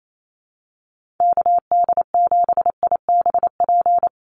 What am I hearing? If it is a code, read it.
Morse code, heard as a keyed tone, about 28 wpm, 700 Hz. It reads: KD7IBP
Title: USA Call Signs Repeated 3x 28wpm